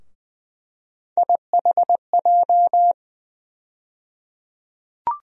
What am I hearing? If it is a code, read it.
Morse code, heard as a keyed tone, about 20 wpm, 700 Hz. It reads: IHJ
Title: USA Call Sign Postfixes - 3 Letters 20wpm